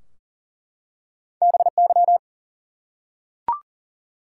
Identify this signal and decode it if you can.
Morse code, heard as a keyed tone, about 39 wpm, 700 Hz. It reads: BY
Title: Top 180 QSO Elements 40wpm